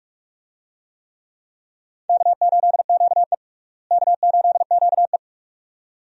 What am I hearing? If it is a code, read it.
Morse code, heard as a keyed tone, about 42 wpm, 700 Hz. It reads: K8QE K8QE
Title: USA Call Signs - 4 Characters - Repeated 2x - 45wpm